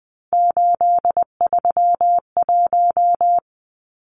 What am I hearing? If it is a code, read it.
Morse code, heard as a keyed tone, about 20 wpm, 700 Hz. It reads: 831
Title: Numbers - 3-Digit 20wpm